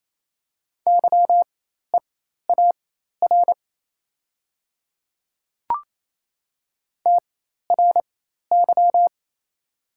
Morse code, heard as a keyed tone, about 28 wpm, spaced as for 13 wpm, 700 Hz. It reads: YEAR TRY
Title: 4x Character Spacing - Top 500 Words - 28wpm